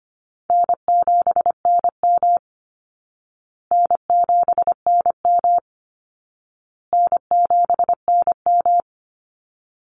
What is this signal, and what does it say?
Morse code, heard as a keyed tone, about 25 wpm, 700 Hz. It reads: N7NM N7NM N7NM
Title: Call Signs Repeated 2x - 25wpm